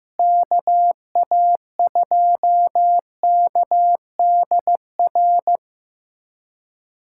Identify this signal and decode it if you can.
Morse code, heard as a keyed tone, about 15 wpm, 700 Hz. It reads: KA2KDR
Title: USA Callsign DE Callsign 15wpm